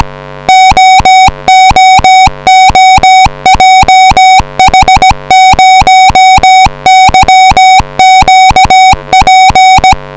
Morse code, heard as a keyed tone, about 17 wpm, 750 Hz. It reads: OOOJH0YQP
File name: eme 5g / hjc / 8n1eme-3